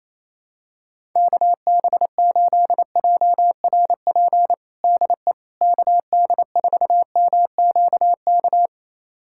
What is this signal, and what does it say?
Morse code, heard as a keyed tone, about 28 wpm, 700 Hz. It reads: KB8JRP DE KD4MQK
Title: USA Callsign DE Callsign - 6 Characters - 28wpm